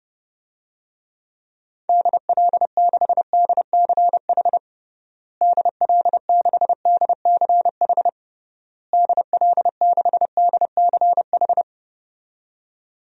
Morse code, heard as a keyed tone, about 32 wpm, 700 Hz. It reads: DL6DCH DL6DCH DL6DCH
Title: DXCC-100 Call Signs - 6 Characters - Repeated 3x - 30wpm